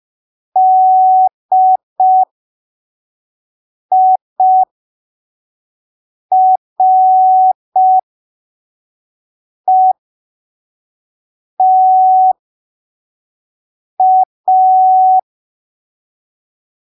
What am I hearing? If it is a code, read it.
Morse code, heard as a keyed tone, about 5 wpm, spaced as for 3 wpm, 750 Hz. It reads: DIRETA
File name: arquivos / texto / texto 01 5ppm